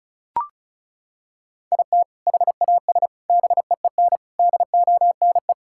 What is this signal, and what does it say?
Morse code, heard as a keyed tone, about 36 wpm, 700 Hz. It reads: IT HAS BEEN DONE
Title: Sentences from Top 1600 Words 35wpm